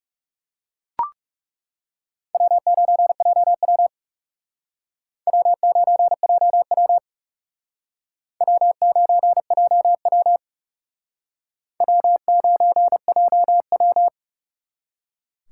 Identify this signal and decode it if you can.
Morse code, heard as a keyed tone, about 38 wpm, 700 Hz. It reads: W9JW W9JW W9JW W9JW
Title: Speed Racing - USA 1x2 Call Signs 30wpm